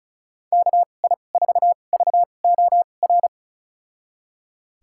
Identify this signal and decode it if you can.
Morse code, heard as a keyed tone, about 34 wpm, spaced as for 25 wpm, 700 Hz. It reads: KI4VOR
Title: USA Call Signs - 6 Characters - 2x Character Spacing - 35wpm